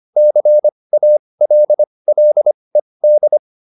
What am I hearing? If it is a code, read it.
Morse code, heard as a keyed tone, about 25 wpm, spaced as for 20 wpm, 600 Hz. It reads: CALLED